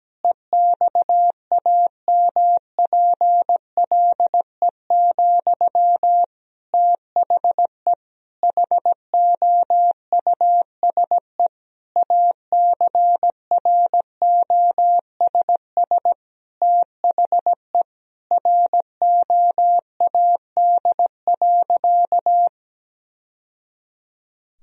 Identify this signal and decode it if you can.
Morse code, heard as a keyed tone, about 17 wpm, 700 Hz. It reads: EXAMPLE, THE HOUSE ACROSS THE ROAD.
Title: Sentences from Top 1000 Words - Review Part 1 - 17wpm